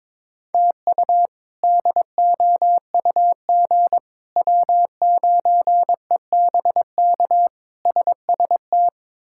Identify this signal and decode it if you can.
Morse code, heard as a keyed tone, about 22 wpm, 700 Hz. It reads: TU DOUG W9EBK SST
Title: K1USN SST QSO Practice - 22wpm